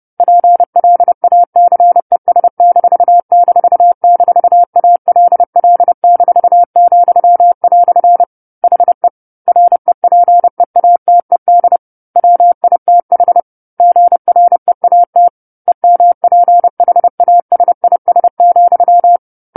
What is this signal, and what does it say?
Morse code, heard as a keyed tone, about 30 wpm, 700 Hz. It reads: PLACES---ALL-," HE REPEATED WITH GREAT EMPHASIS,
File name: 10-Chapter-10 30 WPM